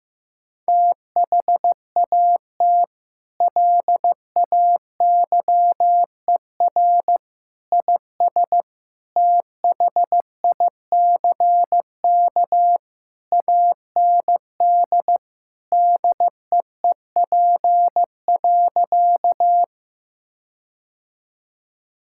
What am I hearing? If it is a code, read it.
Morse code, heard as a keyed tone, about 15 wpm, 700 Hz. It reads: THAT LAYER IS THICK AND DEEP.
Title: Sentences from Top 1600 Words 15wpm